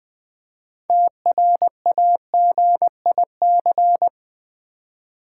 Morse code, heard as a keyed tone, about 20 wpm, 700 Hz. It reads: TRAGIC
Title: 6 Letter Words - First 3 Letters Given - 20wpm